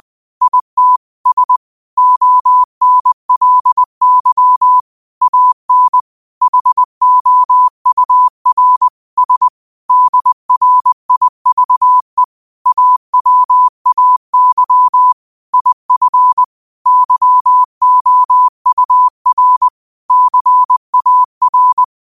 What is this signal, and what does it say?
Morse code, heard as a keyed tone, about 20 wpm, 1000 Hz. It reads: IT S ONLY AN HOUR S DRIVE AWAY IF YOUR CAR
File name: qotdfiles / 20wpm